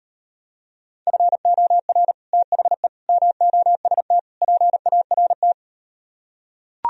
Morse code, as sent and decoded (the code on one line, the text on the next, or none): ..-. --- .-. / - .... / -- --- ... - / .--. .- .-. -
FOR THE MOST PART